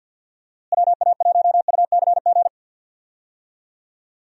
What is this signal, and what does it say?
Morse code, heard as a keyed tone, about 49 wpm, 700 Hz. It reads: WA1UCG
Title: USA Call Signs - 6 Characters - Repeated 2x - 50wpm